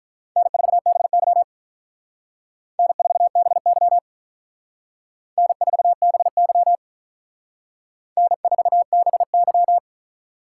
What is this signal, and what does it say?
Morse code, heard as a keyed tone, about 45 wpm, 700 Hz. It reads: N4BY N4BY N4BY N4BY